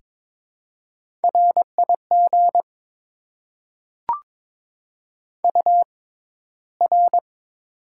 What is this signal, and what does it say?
Morse code, heard as a keyed tone, about 22 wpm, spaced as for 14 wpm, 700 Hz. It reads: RIG UR
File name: Top 90 QSO Elements - 6x Character Spacing - 22wpm